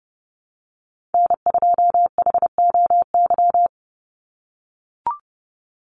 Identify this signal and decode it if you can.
Morse code, heard as a keyed tone, about 30 wpm, 700 Hz. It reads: N2HOY
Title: USA Call Signs 30wpm